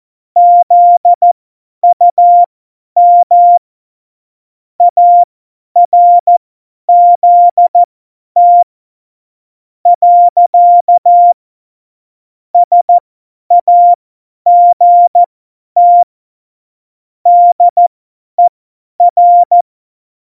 Morse code, heard as a keyed tone, about 14 wpm, spaced as for 9 wpm, 700 Hz. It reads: ZUM ARZT. SAGT DER
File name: Morsekurs / Witze / Witz 12. Dezember 2017, 50 F70